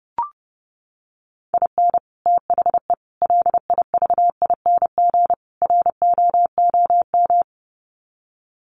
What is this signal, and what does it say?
Morse code, heard as a keyed tone, about 30 wpm, 700 Hz. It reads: IN THE LIVING ROOM
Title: Sets of 4 Words 30wpm